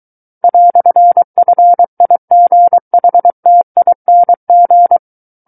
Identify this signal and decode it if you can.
Morse code, heard as a keyed tone, about 23 wpm, 700 Hz. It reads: "FIGHTING
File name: Book One / 14-Chapter 14 - 23 WPM